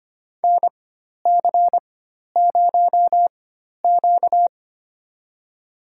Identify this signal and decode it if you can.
Morse code, heard as a keyed tone, about 25 wpm, spaced as for 12 wpm, 700 Hz. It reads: NC0Q
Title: USA Call Signs - 4 Characters - 4x Character Spacing - 25wpm